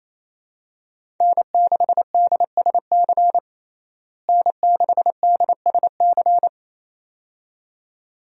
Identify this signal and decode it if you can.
Morse code, heard as a keyed tone, about 28 wpm, 700 Hz. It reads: N6DSC N6DSC
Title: USA Call Signs - 5 Characters - Repeated 2x - 28wpm